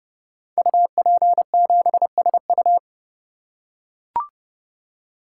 Morse code, heard as a keyed tone, about 30 wpm, 700 Hz. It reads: UP7SU
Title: Code Groups - 5 Letter:Numbers - 30wpm